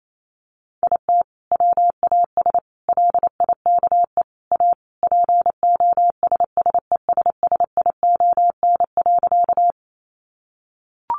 Morse code, heard as a keyed tone, about 28 wpm, 700 Hz. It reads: IT WAS LIKE A POSSESSION.
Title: Sentences from Top 1400 Words 28wpm